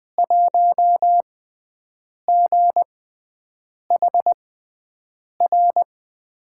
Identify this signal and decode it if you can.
Morse code, heard as a keyed tone, about 20 wpm, spaced as for 7 wpm, 700 Hz. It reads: 1GHR
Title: USA Call Signs - 5 Characters - 6x Character Spacing - 20wpm